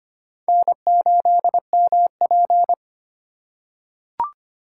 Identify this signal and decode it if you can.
Morse code, heard as a keyed tone, about 25 wpm, 700 Hz. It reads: N8MP
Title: USA Call Signs - 4 Characters - 2x Character Spacing - 25wpm